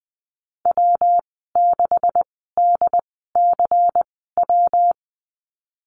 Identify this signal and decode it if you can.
Morse code, heard as a keyed tone, about 20 wpm, spaced as for 15 wpm, 700 Hz. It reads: W6DCW
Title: 140 Lesson 23g - Letter f - Call Signs 20wpm